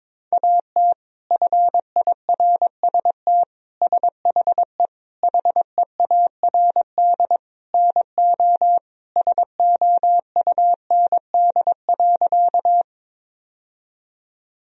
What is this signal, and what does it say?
Morse code, heard as a keyed tone, about 22 wpm, 700 Hz. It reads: AT FIRST SHE HEARD NO SOUND.
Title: Sentences from Top 1000 Words - Review Part 1 - 22wpm